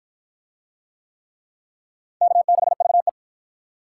Morse code, heard as a keyed tone, about 53 wpm, 700 Hz. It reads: K6VE